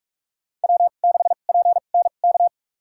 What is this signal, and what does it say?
Morse code, heard as a keyed tone, about 41 wpm, spaced as for 31 wpm, 700 Hz. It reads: W6PNK
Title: USA Call Signs - 5 Characters - 2x Character Spacing - 45wpm